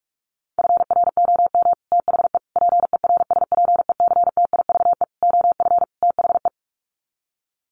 Morse code, heard as a keyed tone, about 44 wpm, 700 Hz. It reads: FROM THE PERSPECTIVE OF THE
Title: Sets of 5 Words Encore2 45wpm